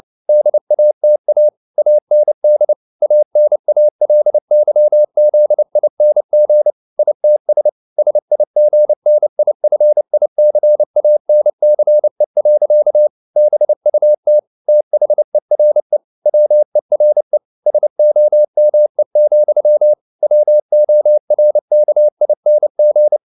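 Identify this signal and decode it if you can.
Morse code, heard as a keyed tone, about 29 wpm, 600 Hz. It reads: DATA AND ANALYZING ITS SIGNIFICANCE. BUT THERE WERE SOME, WORKING
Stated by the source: Morse code at 29 wpm, sending Ken Follett's novel Never in English, from CW med Gnister.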